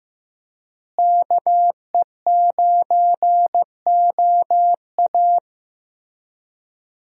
Morse code, heard as a keyed tone, about 15 wpm, 700 Hz. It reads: KE9OA
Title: USA Call Signs - 5 Characters - Repeated 3x - 15wpm